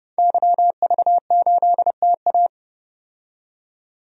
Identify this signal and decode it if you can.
Morse code, heard as a keyed tone, about 30 wpm, 700 Hz. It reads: YV8TA
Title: DXCC-100 Call Signs - 5 Characters - 30wpm